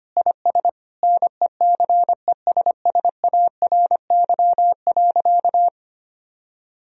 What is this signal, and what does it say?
Morse code, heard as a keyed tone, about 25 wpm, 700 Hz. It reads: IS NECESSARY.
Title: Sentences from Top 1000 Words - Review Part 4 - 25wpm